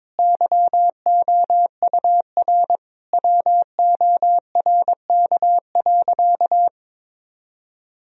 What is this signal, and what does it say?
Morse code, heard as a keyed tone, about 22 wpm, 700 Hz. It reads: YOUR WORK.
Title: Sentences from Top 1000 Words - Review Part 4 - 22wpm